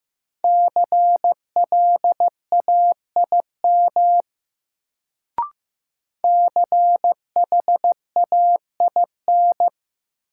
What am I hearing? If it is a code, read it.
Morse code, heard as a keyed tone, about 15 wpm, 700 Hz. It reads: CLAIM CHAIN
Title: Sentences from Top 1500 Words - New Word Drill Alt - 15wpm